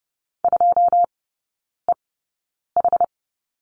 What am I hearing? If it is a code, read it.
Morse code, heard as a keyed tone, about 30 wpm, spaced as for 9 wpm, 690 Hz. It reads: 2EH
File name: Lesson 16e - Letter c - Three Characters 30wpm